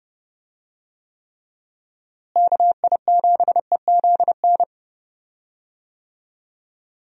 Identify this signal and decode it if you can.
Morse code, heard as a keyed tone, about 32 wpm, 700 Hz. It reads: KI7EZN
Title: USA Call Signs - 6 Characters - Repeated 2x - 30wpm